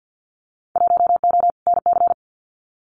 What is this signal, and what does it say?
Morse code, heard as a keyed tone, about 50 wpm, 700 Hz. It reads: 1O NC
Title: Winter Field Day Exchanges 50wpm